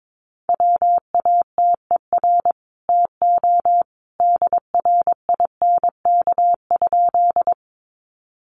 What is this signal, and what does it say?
Morse code, heard as a keyed tone, about 22 wpm, 700 Hz. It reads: WATER TO DRINK?
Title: Sentences from Top 1500 Words 22wpm